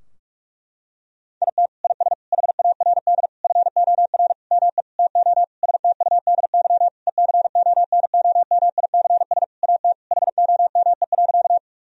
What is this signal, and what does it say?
Morse code, heard as a keyed tone, about 46 wpm, 700 Hz. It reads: IT IS HARD FOR ME TO STUDY ECONOMICS AT HOME.
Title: Sentences from Top 1000 Words - Review Part 3 - 45wpm